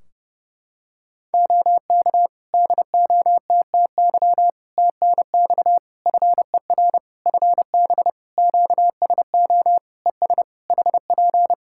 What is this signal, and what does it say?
Morse code, heard as a keyed tone, about 30 wpm, 700 Hz. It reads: OK DOTTY TNX FER FB QSO ES HP
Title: K1USN SST QSO Practice 30wpm